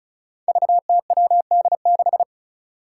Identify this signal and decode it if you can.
Morse code, heard as a keyed tone, about 35 wpm, 700 Hz. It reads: VTWD6
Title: Five Challenging Letter-Number Combinations 35wpm